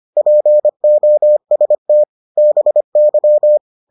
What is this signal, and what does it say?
Morse code, heard as a keyed tone, about 25 wpm, 600 Hz. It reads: POST BY